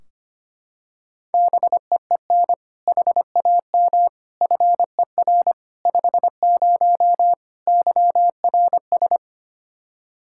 Morse code, heard as a keyed tone, about 25 wpm, 700 Hz. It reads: BEEN HAM FER 50 YRS
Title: K1USN SST QSO Practice 25wpm